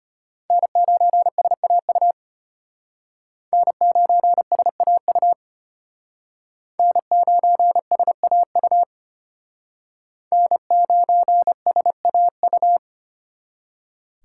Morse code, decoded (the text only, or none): N9SAU N9SAU N9SAU N9SAU